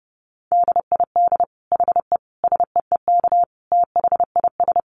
Morse code, heard as a keyed tone, about 30 wpm, 700 Hz. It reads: DID HE SEEK THIS
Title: Dit Challenge 30wpm